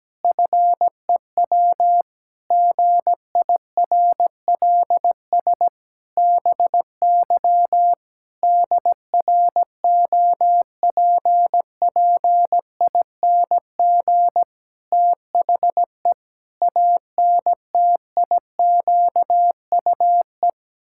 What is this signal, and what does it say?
Morse code, heard as a keyed tone, about 17 wpm, 700 Hz. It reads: FEW GIRLS BY DROPPING THE ANTIQUE
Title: Pangram Challenge v2 - 17wpm